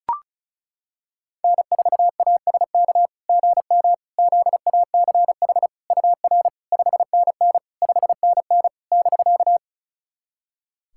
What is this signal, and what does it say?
Morse code, heard as a keyed tone, about 36 wpm, 690 Hz. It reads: N4ASK GM ZACH UR 5NN 5NN <BK>
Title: SOTA QSO Practice 35wpm